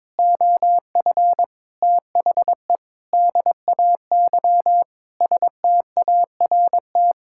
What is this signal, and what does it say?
Morse code, heard as a keyed tone, about 22 wpm, 700 Hz. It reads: OF THE DAY START
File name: Sentences with 7-Word Phrases - 22wpm